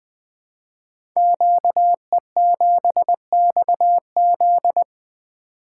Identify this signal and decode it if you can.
Morse code, heard as a keyed tone, about 20 wpm, 700 Hz. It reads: QE7XZ
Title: Code Groups - 5 Letter:Numbers - 20wpm